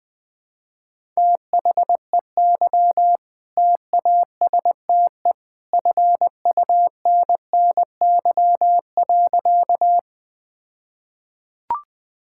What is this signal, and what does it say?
Morse code, heard as a keyed tone, about 20 wpm, 700 Hz. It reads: THEY TASTE FUNNY.
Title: All-Age Appropriate Jokes - 20wpm